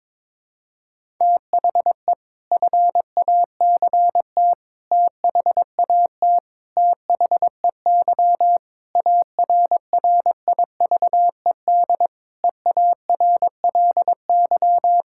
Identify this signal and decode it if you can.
Morse code, heard as a keyed tone, about 22 wpm, 700 Hz. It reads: THE FACT THAT THEY ARRIVED EARLY
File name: practice / 22wpm / Sentences with 4-Word Phrases - 22wpm